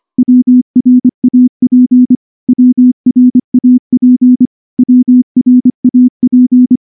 Morse code, heard as a keyed tone, about 24 wpm, 250 Hz. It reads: WRAP WRAP WRAP
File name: morse-training-25WPM